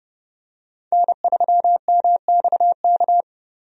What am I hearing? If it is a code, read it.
Morse code, heard as a keyed tone, about 31 wpm, 700 Hz. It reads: N3MXK
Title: USA Call Signs - 5 Characters - 30wpm